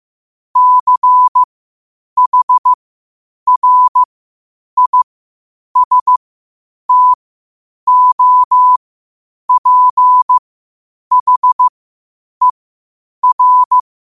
Morse code, heard as a keyed tone, about 15 wpm, spaced as for 8 wpm, 1000 Hz. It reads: CHRISTOPHER
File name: qotdfiles / 8wpm